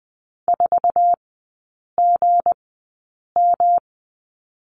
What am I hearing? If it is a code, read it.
Morse code, heard as a keyed tone, about 20 wpm, spaced as for 8 wpm, 700 Hz. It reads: 4GM
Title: 172 Lesson 28e - Number 9 - Three Characters 20wpm